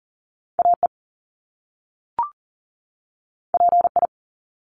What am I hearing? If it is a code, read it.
Morse code, heard as a keyed tone, about 38 wpm, 700 Hz. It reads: AE PI
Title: Top Two Letter Combinations in Words 40wpm